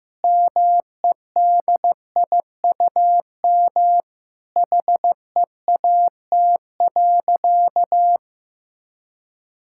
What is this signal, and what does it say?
Morse code, heard as a keyed tone, about 15 wpm, 700 Hz. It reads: MEDIUM HEAT.
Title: Sentences from Top 1000 Words - Review Part 6 - 15wpm